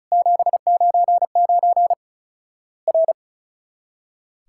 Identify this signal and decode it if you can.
Morse code, heard as a keyed tone, about 34 wpm, 700 Hz. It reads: 799 R
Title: Worked All Europe DX Contest - QTC Exchange - 35wpm